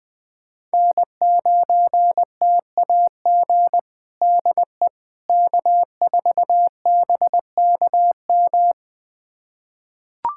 Morse code, heard as a keyed tone, about 21 wpm, 700 Hz. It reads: N9TAG DE K4BKM
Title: USA Callsign DE Callsign - 5 Characters - 20wpm